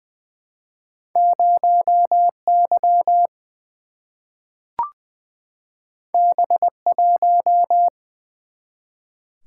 Morse code, heard as a keyed tone, about 20 wpm, 700 Hz. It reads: 0 Y B 1
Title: Single-Letter Single-Number 20wpm